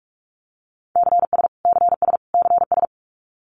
Morse code, heard as a keyed tone, about 44 wpm, 700 Hz. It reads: CS CS CS
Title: Abbreviations Repeated 3x - 45wpm